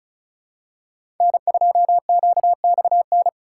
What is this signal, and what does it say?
Morse code, heard as a keyed tone, about 36 wpm, 700 Hz. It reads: N2QXN
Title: USA Call Signs - 5 Characters - 4x Character Spacing - 35wpm